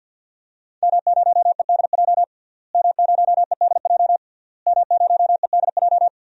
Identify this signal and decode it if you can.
Morse code, heard as a keyed tone, about 53 wpm, 700 Hz. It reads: M0EDJ M0EDJ M0EDJ
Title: DXCC-100 Call Signs - 5 Characters - Repeated 3x - 50wpm